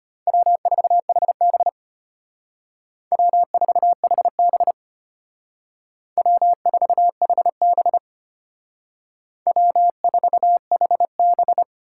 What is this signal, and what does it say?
Morse code, heard as a keyed tone, about 33 wpm, 700 Hz. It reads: W4HB W4HB W4HB W4HB